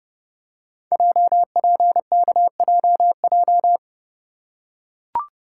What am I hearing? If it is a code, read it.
Morse code, heard as a keyed tone, about 31 wpm, 700 Hz. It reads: JPKJJ
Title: Code Groups - 5 Letter:Numbers - 30wpm